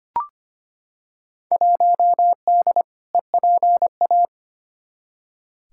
Morse code, heard as a keyed tone, about 25 wpm, 700 Hz. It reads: 1D EPA
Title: Field Day Exchanges - 25wpm